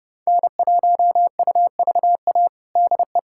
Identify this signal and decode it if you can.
Morse code, heard as a keyed tone, about 29 wpm, 700 Hz. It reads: N1UVA DE
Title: USA Callsign DE Callsign - 5 Characters - 30wpm